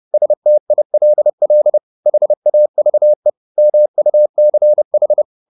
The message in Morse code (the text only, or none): STILL HAVE MUCH